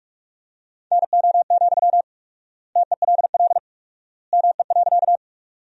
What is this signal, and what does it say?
Morse code, heard as a keyed tone, about 46 wpm, 700 Hz. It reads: NO, TELL ME.
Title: 4x Word-Spacing - Sentences from Top 200 Words - 45wpm